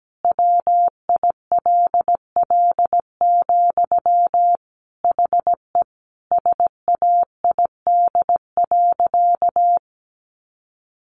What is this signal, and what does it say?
Morse code, heard as a keyed tone, about 17 wpm, 700 Hz. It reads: WILL, HE SAID.
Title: Sentences from Top 200 Words 17wpm